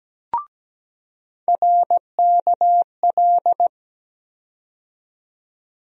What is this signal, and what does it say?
Morse code, heard as a keyed tone, about 17 wpm, 700 Hz. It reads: RKL
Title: USA Call Sign Postfixes - 3 Letters 17wpm